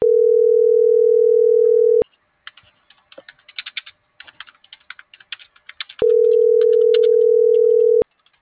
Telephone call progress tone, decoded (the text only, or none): ringback tone